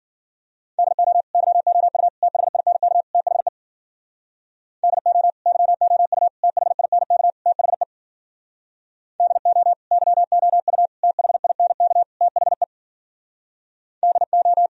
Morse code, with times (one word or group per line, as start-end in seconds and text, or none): 0.8-1.2 DO
1.3-2.1 YOU
2.2-3.0 THINK
3.1-3.5 THE
4.8-5.3 DO
5.5-6.3 YOU
6.4-7.3 THINK
7.5-7.8 THE
9.2-9.7 DO
9.9-10.9 YOU
11.0-12.0 THINK
12.2-12.6 THE
14.0-14.7 DO